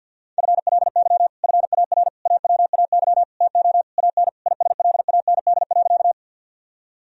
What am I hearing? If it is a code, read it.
Morse code, heard as a keyed tone, about 51 wpm, 700 Hz. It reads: FLY FAR AWAY TO AN ISLAND.
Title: Sentences from Top 1000 Words - Review Part 2 - 50wpm